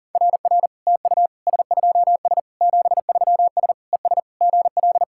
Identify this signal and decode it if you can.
Morse code, heard as a keyed tone, about 40 wpm, 700 Hz. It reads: RR TU S2S 73S ES GL
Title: SOTA QSO Practice 40wpm